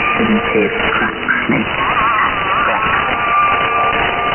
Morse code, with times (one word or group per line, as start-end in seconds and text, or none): none